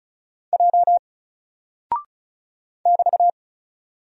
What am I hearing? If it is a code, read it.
Morse code, heard as a keyed tone, about 36 wpm, 700 Hz. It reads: J <BT>
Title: Single Character (Letter:Number:Punctuation:Prosign) Alt 35wpm